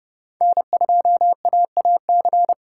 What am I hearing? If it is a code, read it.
Morse code, heard as a keyed tone, about 30 wpm, 700 Hz. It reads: N2AAC
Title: USA Call Signs - 5 Characters - 2x Character Spacing - 30wpm